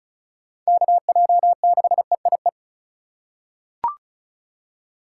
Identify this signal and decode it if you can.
Morse code, heard as a keyed tone, about 36 wpm, 700 Hz. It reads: KJ6EIE
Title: USA Call Signs - 2x Character Spacing - 35wpm